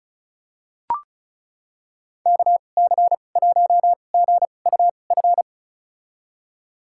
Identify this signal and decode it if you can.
Morse code, heard as a keyed tone, about 34 wpm, spaced as for 25 wpm, 700 Hz. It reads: KC1GUF
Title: USA Call Signs - 2x Character Spacing - 35wpm